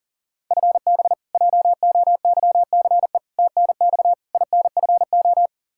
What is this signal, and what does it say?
Morse code, heard as a keyed tone, about 40 wpm, 700 Hz. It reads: FB JOYCE TNX INFO
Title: K1USN SST QSO Practice 40wpm